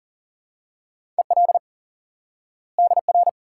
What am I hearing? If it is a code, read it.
Morse code, heard as a keyed tone, about 40 wpm, 700 Hz. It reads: EL DR
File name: IQE Course 3 - Lesson 89 - YAGI - Two Elements - 40wpm